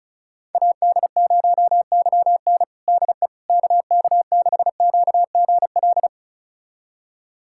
Morse code, heard as a keyed tone, about 36 wpm, 700 Hz. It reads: AD0YN DE KK6QGL
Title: USA Callsign DE Callsign 35wpm